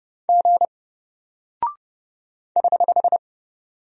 Morse code, heard as a keyed tone, about 30 wpm, 700 Hz. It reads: G <HH>
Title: Single Character (Letter:Number:Punctuation:Prosign) Alt 30wpm